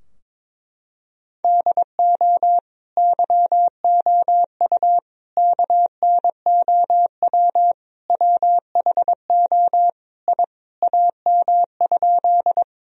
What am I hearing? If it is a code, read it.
Morse code, heard as a keyed tone, about 22 wpm, 700 Hz. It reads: DO YOU KNOW WHO I AM?
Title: Sentences from Top 1000 Words - Review Part 2 - 22wpm